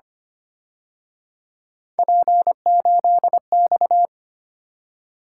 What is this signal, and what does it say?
Morse code, heard as a keyed tone, about 29 wpm, 700 Hz. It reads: P8X
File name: Speed-Racing - Three Letter-Number Combinations - 25wpm